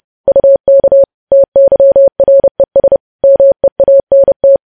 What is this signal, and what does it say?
Morse code, heard as a keyed tone, about 30 wpm, 550 Hz. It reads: UK TYRES MEANT
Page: news headlines in morse code at 30 wpm